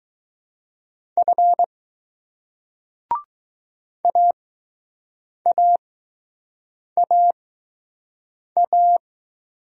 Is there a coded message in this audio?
Yes: Morse code from 1.2 to 9.0 s, heard as a keyed tone, about 20 wpm, spaced as for 6 wpm, 700 Hz. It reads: F AAAA